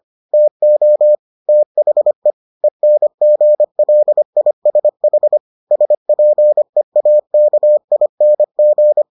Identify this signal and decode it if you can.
Morse code, heard as a keyed tone, about 25 wpm, 600 Hz. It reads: TO THE ENGLISH SPEAKING